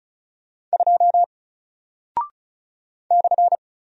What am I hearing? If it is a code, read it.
Morse code, heard as a keyed tone, about 36 wpm, 700 Hz. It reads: 2 /
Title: Single Character (Letter:Number:Punctuation:Prosign) Alt 35wpm